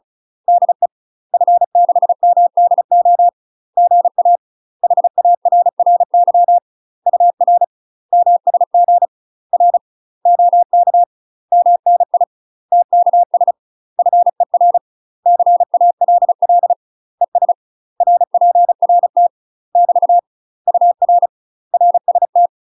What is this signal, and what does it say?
Morse code, heard as a keyed tone, about 34 wpm, 700 Hz. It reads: DE F6MDO GA HARRY UR MSG R OK MNI TKS FER CALL ES RPRT <BT> UR RST
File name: UK MASTER QSOs TRAINING / 35WPM / UK MASTER QSOs 0008